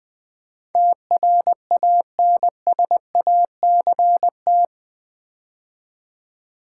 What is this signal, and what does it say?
Morse code, heard as a keyed tone, about 20 wpm, 700 Hz. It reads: TRANSACT
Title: Build A Word - 4 Letters At A Time - 11-Letter Words - 20wpm